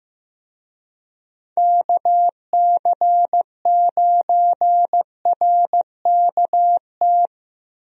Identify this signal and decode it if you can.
Morse code, heard as a keyed tone, about 15 wpm, 700 Hz. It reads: KC9RKT
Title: USA Call Signs - 6 Characters - Repeated 2x - 15wpm